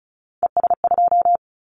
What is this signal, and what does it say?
Morse code, heard as a keyed tone, about 36 wpm, 700 Hz. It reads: ES2
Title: Callsign Prefixes for Top 100 DXCC - 35wpm